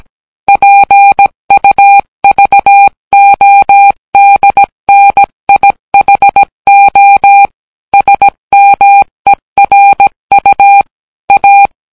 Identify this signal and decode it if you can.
Morse code, heard as a keyed tone, about 17 wpm, 800 Hz. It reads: PUVODNIHO SMERU A